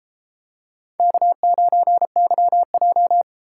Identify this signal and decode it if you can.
Morse code, heard as a keyed tone, about 28 wpm, 700 Hz. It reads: K9YJ